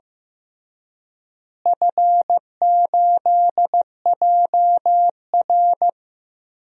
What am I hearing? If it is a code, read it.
Morse code, heard as a keyed tone, about 16 wpm, 700 Hz. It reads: F8JR
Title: DXCC-100 Call Signs - 4 Characters - Repeated 3x - 15wpm